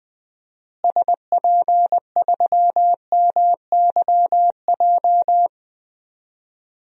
Morse code, heard as a keyed tone, about 20 wpm, 700 Hz. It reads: SP3MYJ
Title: DXCC-100 Call Signs - 6 Characters - Repeated 3x - 20wpm